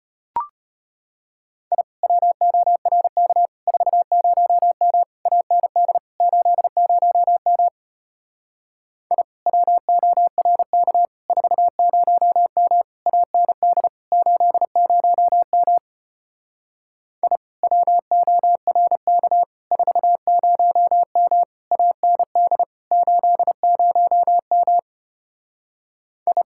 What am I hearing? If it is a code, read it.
Morse code, heard as a keyed tone, about 33 wpm, 700 Hz. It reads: I WORK 40M AND 80M I WORK 40M AND 80M I WORK 40M AND 80M I